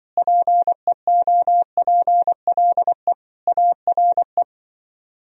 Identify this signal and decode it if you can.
Morse code, heard as a keyed tone, about 24 wpm, 700 Hz. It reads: PEOPLE ARE
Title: Speed-Racing - Sets of 5 Words 20wpm